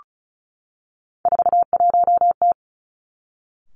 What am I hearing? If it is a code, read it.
Morse code, heard as a keyed tone, about 36 wpm, 700 Hz. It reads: 41T
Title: Three Challenging Letter-Number Combinations 35wpm